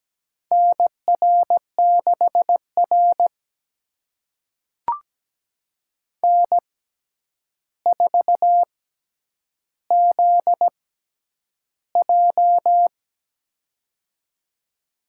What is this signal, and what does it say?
Morse code, heard as a keyed tone, about 17 wpm, spaced as for 9 wpm, 700 Hz. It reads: NR6R N4ZJ
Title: USA Call Signs - 4 Characters - 6x Character Spacing - 17wpm